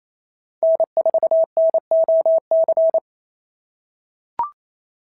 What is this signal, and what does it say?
Morse code, heard as a keyed tone, about 28 wpm, 650 Hz. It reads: N4NOC
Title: SOTA QSO Practice 28wpm